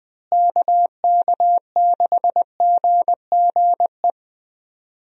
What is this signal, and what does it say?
Morse code, heard as a keyed tone, about 20 wpm, 700 Hz. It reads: KK6GGE
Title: USA Call Signs - 6 Characters - Repeated 2x - 20wpm